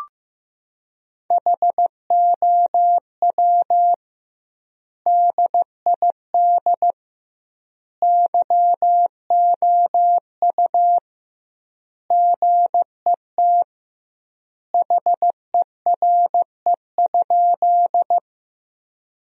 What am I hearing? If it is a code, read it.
Morse code, heard as a keyed tone, about 16 wpm, 700 Hz. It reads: HOW DID YOU GET HERE?
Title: 2x Word-Spacing - Sentences from Top 200 Words - 15wpm